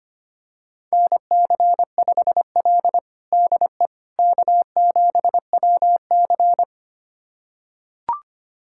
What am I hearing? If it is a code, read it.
Morse code, heard as a keyed tone, about 25 wpm, 700 Hz. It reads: NC5L DE K7WC
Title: USA Callsign DE Callsign - 4 Characters - 25wpm